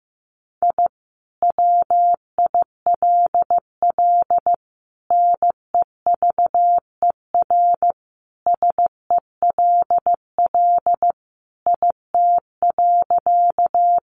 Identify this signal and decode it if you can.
Morse code, heard as a keyed tone, about 15 wpm, 700 Hz. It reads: I WILL NEVER SELL IT.
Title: Sentences from Top 1100 Words 15wpm